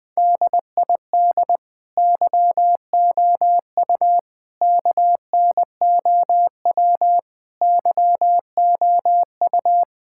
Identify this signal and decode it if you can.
Morse code, heard as a keyed tone, about 20 wpm, 700 Hz. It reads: DID YOU KNOW YOU
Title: Sentences with 9-Word Phrases - Spoken Beginning - 20wpm